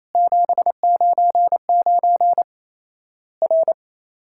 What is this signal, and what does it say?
Morse code, heard as a keyed tone, about 28 wpm, 700 Hz. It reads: 799 R
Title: Worked All Europe DX Contest - QTC Exchange - 28wpm